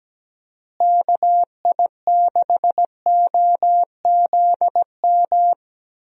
Morse code, heard as a keyed tone, about 17 wpm, 700 Hz. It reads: KI6OZM
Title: USA Call Signs - 6 Characters - Repeated 3x - 17wpm